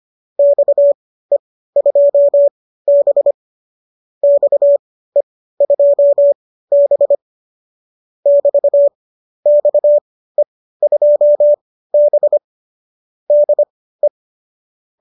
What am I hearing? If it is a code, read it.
Morse code, heard as a keyed tone, about 25 wpm, spaced as for 15 wpm, 590 Hz. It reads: XE2B XE2B <BT> XE2B DE
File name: qso204 15